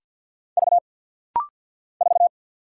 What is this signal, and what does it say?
Morse code, heard as a keyed tone, about 50 wpm, 690 Hz. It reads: V 4
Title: Single Letter-Number Alt - Rapid-Fire 50wpm